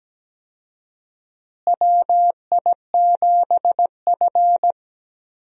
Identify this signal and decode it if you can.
Morse code, heard as a keyed tone, about 17 wpm, 700 Hz. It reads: WI7F
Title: USA Call Signs - 4 Characters - Repeated 3x - 17wpm